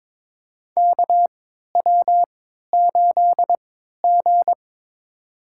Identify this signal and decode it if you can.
Morse code, heard as a keyed tone, about 22 wpm, spaced as for 12 wpm, 700 Hz. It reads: KW8G
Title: USA Call Signs - 4 Characters - 3x Character Spacing - 22wpm